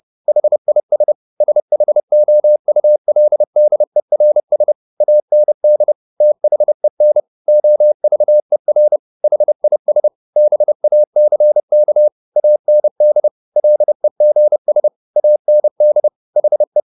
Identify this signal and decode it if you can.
Morse code, heard as a keyed tone, about 30 wpm, 600 Hz. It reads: HIS SHOULDERS AND THEN OVER HIS BACK AND LEGS AND HE